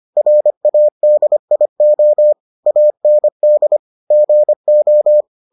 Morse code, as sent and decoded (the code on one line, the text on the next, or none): .-. .- -.. .. --- / .- -. -.. / --. ---
RADIO AND GO